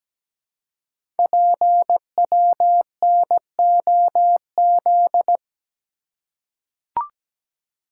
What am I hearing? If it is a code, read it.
Morse code, heard as a keyed tone, about 17 wpm, 700 Hz. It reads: PWNOZ
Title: Code Groups - 5 Letters - 17wpm